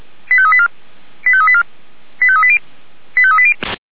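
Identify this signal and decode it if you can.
Five-tone selective call (ZVEI), heard as quick runs of five tones, tones of about 70 ms. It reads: 85384 85384 85380 85380
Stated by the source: ZVEI five-tone selective call received by SDR.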